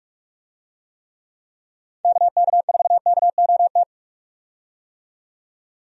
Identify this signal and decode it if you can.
Morse code, heard as a keyed tone, about 43 wpm, 700 Hz. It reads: KK4KOT
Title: USA Call Signs - 6 Characters - 45wpm